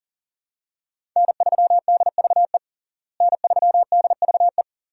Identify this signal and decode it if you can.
Morse code, heard as a keyed tone, about 40 wpm, 700 Hz. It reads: N3DVE N3DVE
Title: USA Call Signs - 5 Characters - Repeated 3x - 40wpm